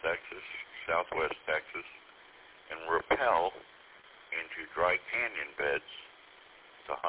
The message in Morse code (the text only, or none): none